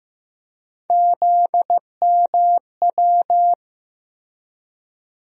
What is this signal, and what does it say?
Morse code, heard as a keyed tone, about 15 wpm, 700 Hz. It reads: ZMW
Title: USA Call Sign Postfixes - 3 Letters 15wpm